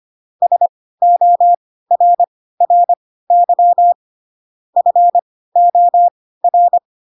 Morse code, heard as a keyed tone, about 25 wpm, spaced as for 16 wpm, 700 Hz. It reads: SORRY FOR
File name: Pinocchio / Pinocchio ch4 10 25 ik1hge